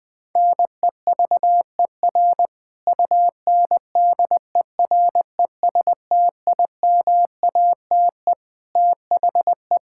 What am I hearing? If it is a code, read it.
Morse code, heard as a keyed tone, about 21 wpm, 700 Hz. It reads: NEVER UNDERESTIMATE THE
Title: Sentences with 4-Word Phrases - 20wpm